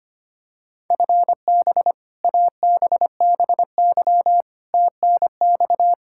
Morse code, heard as a keyed tone, about 25 wpm, 700 Hz. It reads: FB ABBY TNX
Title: Practice QSO - Exchange 3 - More Info - 25wpm